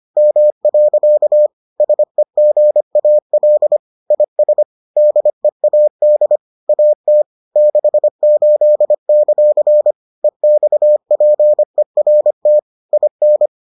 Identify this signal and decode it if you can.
Morse code, heard as a keyed tone, about 25 wpm, 600 Hz. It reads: M. SEGAL IS DEAD AT 68; EXPERT IN